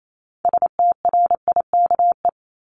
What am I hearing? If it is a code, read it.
Morse code, heard as a keyed tone, about 28 wpm, 700 Hz. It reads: STRIKE
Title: Top 1501-1750 Words 28wpm